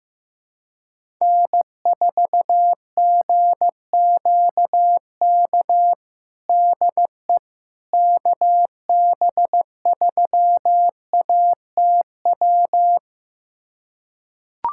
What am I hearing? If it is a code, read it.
Morse code, heard as a keyed tone, about 15 wpm, 700 Hz. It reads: N4GQK DE KB3ATW
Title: USA Callsign DE Callsign 15wpm